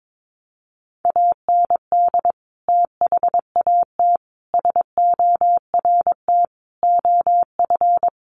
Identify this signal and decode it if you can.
Morse code, heard as a keyed tone, about 22 wpm, 700 Hz. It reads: AND THAT SORT OF
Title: Sets of 5 Words Encore2 22wpm